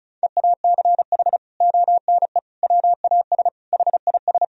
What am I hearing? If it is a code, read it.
Morse code, heard as a keyed tone, about 36 wpm, 700 Hz. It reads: EACH ONE WAS HIS
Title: Sentences from Top 1000 Words - Review Part 4 - 35wpm